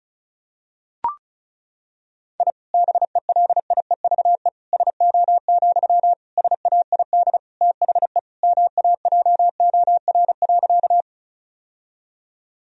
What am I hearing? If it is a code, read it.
Morse code, heard as a keyed tone, about 36 wpm, 700 Hz. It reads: I BELIEVE SO, SAID THE MAJOR.
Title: Sentences from Top 1000 Words - Review Part 4 - 35wpm